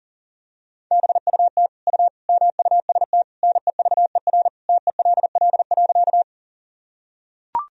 Morse code, heard as a keyed tone, about 40 wpm, 700 Hz. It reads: BUT U MUST NEVER TELL.
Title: Sentences from Top 300 Words and Abbreviations 40wpm